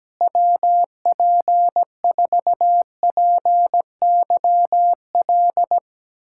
Morse code, heard as a keyed tone, about 17 wpm, 700 Hz. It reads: WP4PYL
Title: USA Callsign DE Callsign - 6 Characters - 17wpm